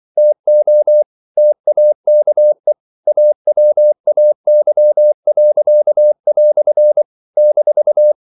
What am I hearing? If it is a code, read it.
Morse code, heard as a keyed tone, about 24 wpm, 600 Hz. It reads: TO TAKE AWAY." -
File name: CW / Practice / EnglishQuotations0224